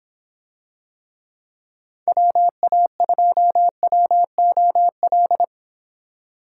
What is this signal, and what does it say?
Morse code, heard as a keyed tone, about 26 wpm, 700 Hz. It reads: WA2WOL